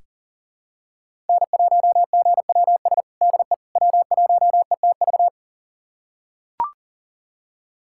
Morse code, heard as a keyed tone, about 40 wpm, 700 Hz. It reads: N1GWS DE W1ETV
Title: USA Callsign DE Callsign - 5 Characters - 40wpm